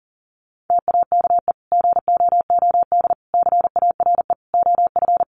Morse code, heard as a keyed tone, about 40 wpm, 700 Hz. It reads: TAKE GOOD CARE OF
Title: Sentences from Top 400 Words 40wpm